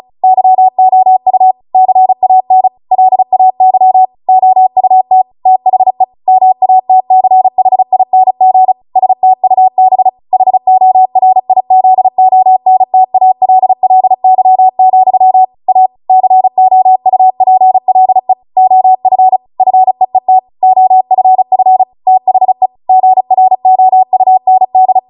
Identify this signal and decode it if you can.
Morse code, heard as a keyed tone, about 36 wpm, 750 Hz. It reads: YOU CAN LAY OUT THE MATCHING STUB HORIZONTALLY, A COUPLE OF FEET OFF THE GROUND